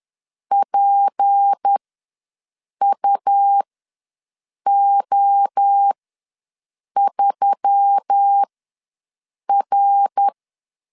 Morse code, heard as a keyed tone, about 10 wpm, spaced as for 6 wpm, 800 Hz. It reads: PUO3R
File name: LEZIONI / 6 PKX0 / 6 P ws 10wpm 3751